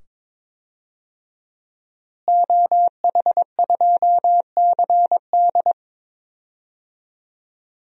Morse code, heard as a keyed tone, about 22 wpm, 700 Hz. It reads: OH2CD
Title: DXCC-100 Call Signs - 5 Characters - Repeated 2x - 22wpm